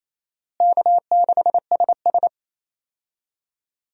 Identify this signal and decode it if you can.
Morse code, heard as a keyed tone, about 28 wpm, 700 Hz. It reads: K6SS
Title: USA Call Signs - 4 Characters - Repeated 2x - 28wpm